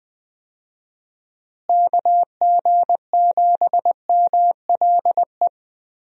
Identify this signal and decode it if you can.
Morse code, heard as a keyed tone, about 20 wpm, 700 Hz. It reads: KG7MLE
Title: USA Call Signs - 6 Characters - Repeated 2x - 20wpm